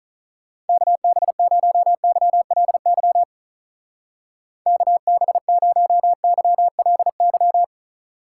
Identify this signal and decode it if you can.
Morse code, heard as a keyed tone, about 43 wpm, 700 Hz. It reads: KB0YLY KB0YLY
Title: Speed Racing - USA 2x3 Call Signs 35wpm